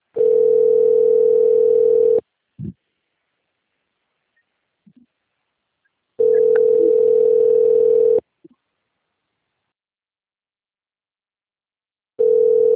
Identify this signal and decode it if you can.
Ringback tone (North America), 440 + 480 Hz.
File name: rg-1600-unknown- - - .34864